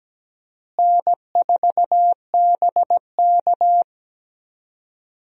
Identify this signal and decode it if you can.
Morse code, heard as a keyed tone, about 17 wpm, 700 Hz. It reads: N4BK
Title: USA Call Signs - 4 Characters - Repeated 2x - 17wpm